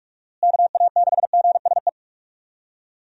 Morse code, heard as a keyed tone, about 46 wpm, 700 Hz. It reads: KA6GSE
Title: USA Call Signs - 6 Characters - Repeated 3x - 45wpm